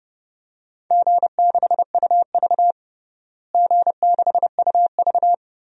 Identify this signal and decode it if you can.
Morse code, heard as a keyed tone, about 32 wpm, 700 Hz. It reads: G6UV G6UV
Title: DXCC-100 Call Signs - 4 Characters - Repeated 3x - 30wpm